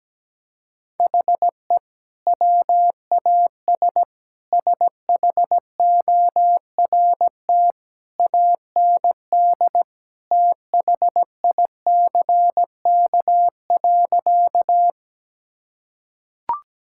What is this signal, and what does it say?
Morse code, heard as a keyed tone, about 17 wpm, 700 Hz. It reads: HE WAS SHORT AND THICK.
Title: Sentences from Top 1600 Words 17wpm